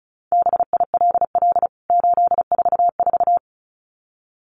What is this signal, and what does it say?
Morse code, heard as a keyed tone, about 34 wpm, 700 Hz. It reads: BILL 844
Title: CWT Exchange 35wpm